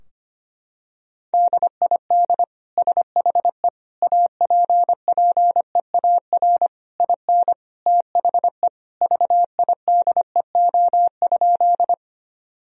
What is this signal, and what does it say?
Morse code, heard as a keyed tone, about 25 wpm, 700 Hz. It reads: DID SHE APPEAR IN THE VIDEO?
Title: Sentences from Top 1000 Words - Review Part 4 - 25wpm